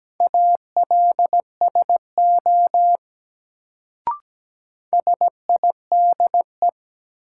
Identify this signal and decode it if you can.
Morse code, heard as a keyed tone, about 17 wpm, 700 Hz. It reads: ALSO SIDE
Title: Sentences from Top 400 Words - New Word Drill - 17wpm